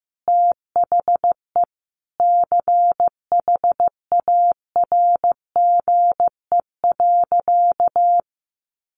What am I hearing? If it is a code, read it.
Morse code, heard as a keyed tone, about 14 wpm, 700 Hz. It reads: THE CHARGE.
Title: Sentences from Top 1300 Words 15wpm